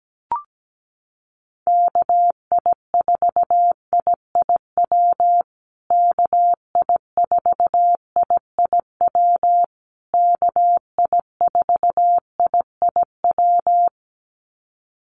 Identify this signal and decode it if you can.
Morse code, heard as a keyed tone, about 17 wpm, 700 Hz. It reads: KI4IIW KI4IIW KI4IIW
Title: USA Call Signs Repeated 3x 17wpm